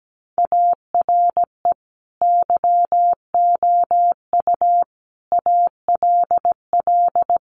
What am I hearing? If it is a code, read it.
Morse code, heard as a keyed tone, about 17 wpm, 700 Hz. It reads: ARE YOU ALL
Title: Sentences from Top 400 Words 17wpm